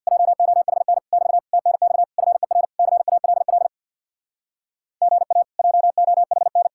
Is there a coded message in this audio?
Yes: Morse code from 0.1 to 6.7 s, heard as a keyed tone, about 64 wpm, 700 Hz. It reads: JOHN <BT> TNX FER CALL GA JOHN